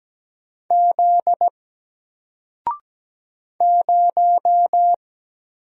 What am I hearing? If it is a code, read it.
Morse code, heard as a keyed tone, about 17 wpm, 700 Hz. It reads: Z 0
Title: Single Letter-Number 17wpm